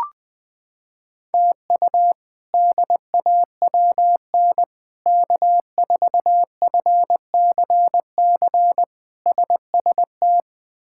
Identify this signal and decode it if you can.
Morse code, heard as a keyed tone, about 20 wpm, 700 Hz. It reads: TU DAWN K4FCC SST
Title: K1USN SST QSO Practice - 20wpm